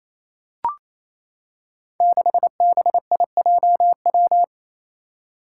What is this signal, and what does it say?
Morse code, heard as a keyed tone, about 27 wpm, 700 Hz. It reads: 6BIJW
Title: Five Challenging Letter-Number Combinations 28wpm